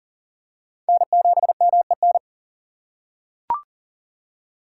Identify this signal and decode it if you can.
Morse code, heard as a keyed tone, about 40 wpm, 700 Hz. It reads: N7MEN
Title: USA Call Signs - 5 Characters - 4x Character Spacing - 40wpm